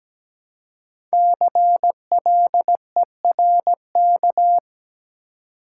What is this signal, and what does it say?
Morse code, heard as a keyed tone, about 17 wpm, 700 Hz. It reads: CLERK
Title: Five Letter Words - 17wpm